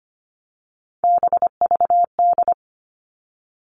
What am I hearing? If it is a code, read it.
Morse code, heard as a keyed tone, about 25 wpm, 700 Hz. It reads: BVD
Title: Three Challenging Letter-Number Combinations 25wpm